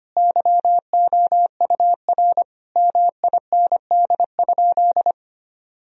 Morse code, heard as a keyed tone, about 25 wpm, 700 Hz. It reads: YOUR MIND?
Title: Sentences from Top 1000 Words - Review Part 2 - 25wpm